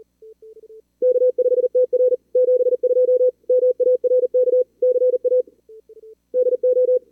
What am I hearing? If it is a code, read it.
Morse code, heard as a keyed tone, about 38 wpm, 440 Hz. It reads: K5TR 72 MARK CA DO
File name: CWSprint / Feb2022 / k5tr-feb-na-sprint-cw-k5ot -2300